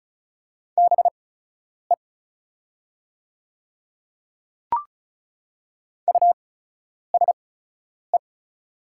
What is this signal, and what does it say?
Morse code, heard as a keyed tone, about 34 wpm, spaced as for 10 wpm, 690 Hz. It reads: BE USE
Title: 8x Character Spacing - Top 300 Words - 35wpm